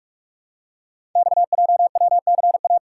Morse code, heard as a keyed tone, about 47 wpm, 700 Hz. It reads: XJWCA
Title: Code Groups - 5 Letter:Numbers - 45wpm